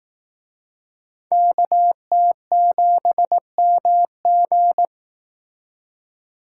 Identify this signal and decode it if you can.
Morse code, heard as a keyed tone, about 18 wpm, 700 Hz. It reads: KT7MG
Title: Speed Racing - USA 2x2 Call Signs 15wpm